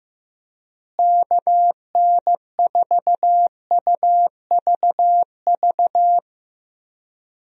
Morse code, heard as a keyed tone, about 15 wpm, 700 Hz. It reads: KN4UVV
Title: USA Call Signs - 6 Characters - Repeated 2x - 15wpm